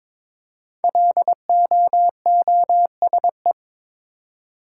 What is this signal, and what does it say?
Morse code, heard as a keyed tone, about 22 wpm, 700 Hz. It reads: LOOSE
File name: 5 Letter Words - First 3 Letters Given - 22wpm